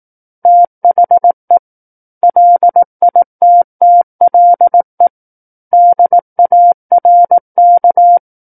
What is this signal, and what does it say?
Morse code, heard as a keyed tone, about 18 wpm, 700 Hz. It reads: THE LITTLE DARK
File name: 18 WPM